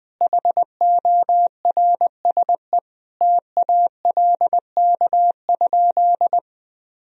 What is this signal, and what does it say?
Morse code, heard as a keyed tone, about 20 wpm, 700 Hz. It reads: HORSE TALK?
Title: Sentences from Top 1000 Words - Review Part 1 - 20wpm